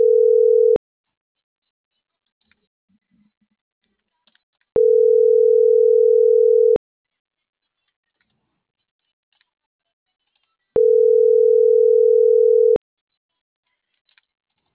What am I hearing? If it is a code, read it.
Ringback tone (North America), 440 + 480 Hz.